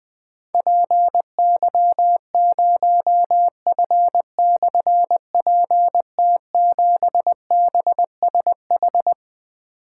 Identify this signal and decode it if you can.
Morse code, heard as a keyed tone, about 20 wpm, 700 Hz. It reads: PY0F/PT7BSH
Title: Difficult Call Signs 20wpm